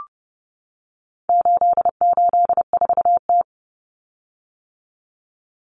Morse code, heard as a keyed tone, about 30 wpm, 700 Hz. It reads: 884T
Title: Common Cut Numbers - 4-Digit 30wpm